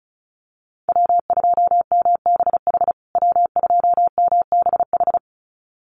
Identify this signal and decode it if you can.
Morse code, heard as a keyed tone, about 34 wpm, 700 Hz. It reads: W2MBH W2MBH
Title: USA Call Signs Repeated 2x 35wpm